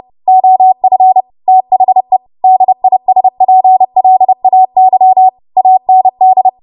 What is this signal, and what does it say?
Morse code, heard as a keyed tone, about 30 wpm, 750 Hz. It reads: OF THE DISPLAY AND
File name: Archive / 30 WPM / 30WPM